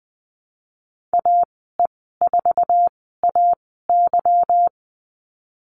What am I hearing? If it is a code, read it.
Morse code, heard as a keyed tone, about 20 wpm, spaced as for 15 wpm, 700 Hz. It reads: AE4AY
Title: 154 Lesson 25g - Letter p - Call Signs 20wpm